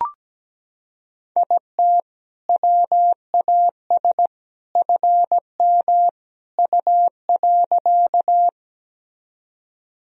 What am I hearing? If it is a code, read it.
Morse code, heard as a keyed tone, about 17 wpm, 700 Hz. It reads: IT WAS FM U.
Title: Sentences from Top 100 Words and Abbreviations 17wpm